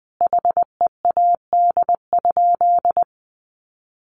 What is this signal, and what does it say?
Morse code, heard as a keyed tone, about 20 wpm, 700 Hz. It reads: HEAD?
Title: Sentences from Top 1100 Words 20wpm